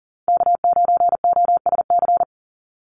Morse code, heard as a keyed tone, about 40 wpm, 700 Hz. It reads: K9OSC
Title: NA SOTA Chaser Callsigns 40wpm